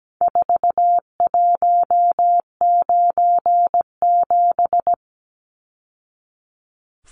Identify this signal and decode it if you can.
Morse code, heard as a keyed tone, about 17 wpm, 700 Hz. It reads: 4197
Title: Numbers - 4-Digit 17wpm